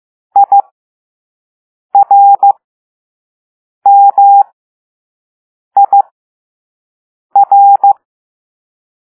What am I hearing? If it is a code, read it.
Morse code, heard as a keyed tone, about 15 wpm, spaced as for 5 wpm, 800 Hz. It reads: IRMIR
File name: Morse / Level2A